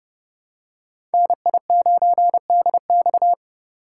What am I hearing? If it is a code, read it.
Morse code, heard as a keyed tone, about 32 wpm, 700 Hz. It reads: NI9DX
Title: USA Call Signs - 3x Character Spacing - 30wpm